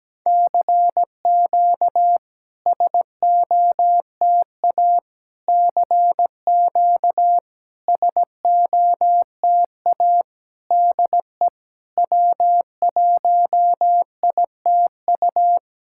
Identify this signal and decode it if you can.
Morse code, heard as a keyed tone, about 17 wpm, 700 Hz. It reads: CQ SOTA CQ SOTA DE W1ITU
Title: SOTA QSO Practice 17wpm